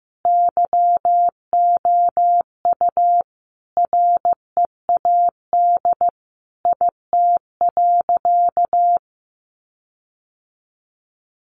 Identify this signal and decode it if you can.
Morse code, heard as a keyed tone, about 15 wpm, 700 Hz. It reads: YOU READ IT.
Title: Sentences from Top 1300 Words 15wpm